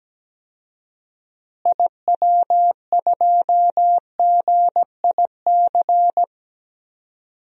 Morse code, heard as a keyed tone, about 17 wpm, 700 Hz. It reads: IW2GIC
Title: DXCC-100 Call Signs - 6 Characters - Repeated 2x - 17wpm